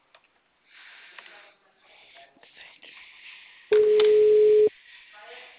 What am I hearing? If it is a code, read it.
Ringback tone (Europe, 425 Hz), with speech.